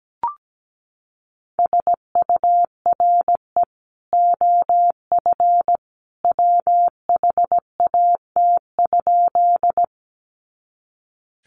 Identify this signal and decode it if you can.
Morse code, heard as a keyed tone, about 17 wpm, 700 Hz. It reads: SURE OF WHAT?
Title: Sentences from Top 300 Words 17wpm